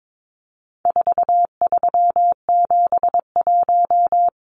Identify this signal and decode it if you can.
Morse code, heard as a keyed tone, about 22 wpm, 700 Hz. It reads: 4371
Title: Numbers - 4-Digit 22wpm